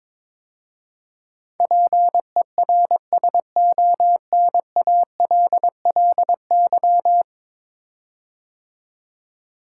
Morse code, heard as a keyed tone, about 22 wpm, 700 Hz. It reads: PERSONALLY
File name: practice / 22wpm / 10 Letter Words - First 3 Letters Given - 22wpm